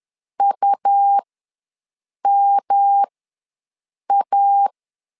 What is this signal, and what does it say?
Morse code, heard as a keyed tone, about 10 wpm, spaced as for 6 wpm, 800 Hz. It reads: UMA
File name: LEZIONI / 3 MWRO / 3 MW ws 10wpm 2213